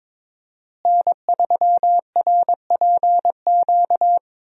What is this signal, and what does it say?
Morse code, heard as a keyed tone, about 22 wpm, 700 Hz. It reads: N3RPQ
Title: USA Call Signs - 5 Characters - Repeated 3x - 22wpm